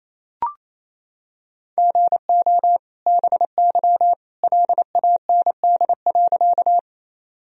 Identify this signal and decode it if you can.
Morse code, heard as a keyed tone, about 28 wpm, 700 Hz. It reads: GO BY LAND.
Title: Sentences from Top 1000 Words - Review Part 1 - 28wpm